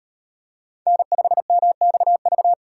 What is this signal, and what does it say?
Morse code, heard as a keyed tone, about 38 wpm, 700 Hz. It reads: N5MXV